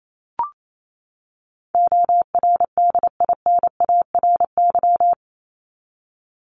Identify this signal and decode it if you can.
Morse code, heard as a keyed tone, about 28 wpm, 700 Hz. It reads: ORDINARY
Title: Top 1751-2000 Words 28wpm